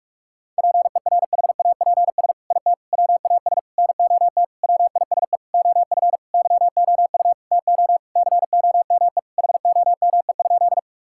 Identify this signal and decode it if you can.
Morse code, heard as a keyed tone, about 46 wpm, 700 Hz. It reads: PERHAPS IT WAS NOT WISE OF YOU TO COME HOME?
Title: Sentences from Top 1600 Words 45wpm